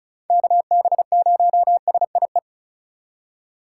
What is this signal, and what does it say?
Morse code, heard as a keyed tone, about 44 wpm, 700 Hz. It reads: KB0SIE